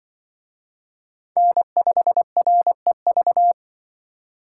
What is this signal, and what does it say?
Morse code, heard as a keyed tone, about 24 wpm, 700 Hz. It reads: N5REV
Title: Speed Racing - USA 1x3 Call Signs 20wpm